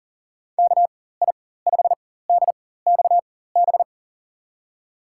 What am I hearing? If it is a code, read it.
Morse code, heard as a keyed tone, about 40 wpm, spaced as for 19 wpm, 700 Hz. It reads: KI5DXB
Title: USA Call Signs - 6 Characters - 4x Character Spacing - 40wpm